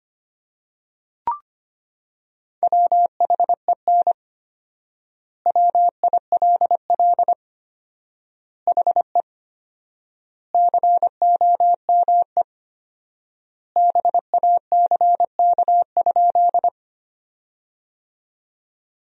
Morse code, heard as a keyed tone, about 25 wpm, 700 Hz. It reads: WHEN WILL HE COME BACK?
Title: 4x Word-Spacing - Sentences from Top 200 Words - 25wpm